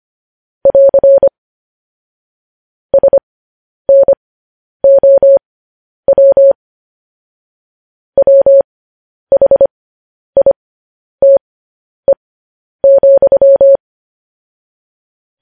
Morse code, heard as a keyed tone, about 25 wpm, spaced as for 10 wpm, 550 Hz. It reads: <AR> SNOW WHITE,
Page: news-headlines-in-morse-code-at-10-wpm